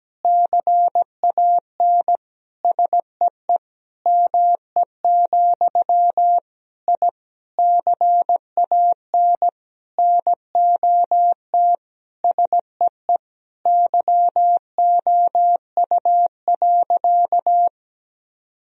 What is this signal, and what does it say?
Morse code, heard as a keyed tone, about 17 wpm, 700 Hz. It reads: CAN SEE ME, I CAN NOT SEE YOU.
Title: Sentences from Top 1000 Words - Review Part 3 - 17wpm